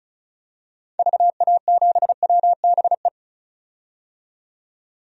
Morse code, heard as a keyed tone, about 36 wpm, 700 Hz. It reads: VA7WBE
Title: DXCC-100 Call Signs - 6 Characters - 35wpm